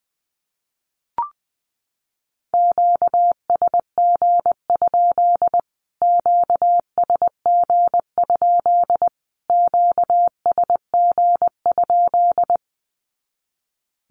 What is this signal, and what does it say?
Morse code, heard as a keyed tone, about 20 wpm, 700 Hz. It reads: QSG? QSG? QSG?
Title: Q-Codes Repeated 3x 20wpm